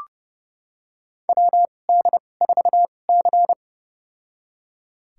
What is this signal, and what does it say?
Morse code, heard as a keyed tone, about 32 wpm, spaced as for 22 wpm, 700 Hz. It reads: WD4C
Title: USA Call Signs - 4 Characters - 2x Character Spacing - 30wpm